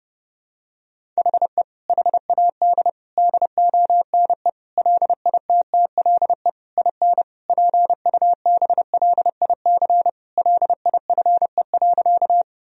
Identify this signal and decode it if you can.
Morse code, heard as a keyed tone, about 32 wpm, 700 Hz. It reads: HE HAD DONE LITTLE IN PUBLIC LIFE.
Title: Sentences from Top 1000 Words - Review Part 3 - 30wpm